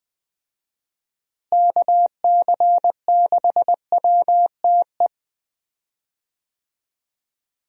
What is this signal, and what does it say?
Morse code, heard as a keyed tone, about 20 wpm, 700 Hz. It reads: KC6WTE
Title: USA Call Signs - 6 Characters - Repeated 2x - 20wpm